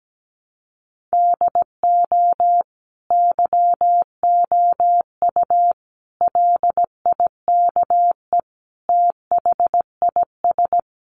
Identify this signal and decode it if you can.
Morse code, heard as a keyed tone, about 17 wpm, 700 Hz. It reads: DO YOU LIKE THIS
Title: Sentences from Top 1200 Words 17wpm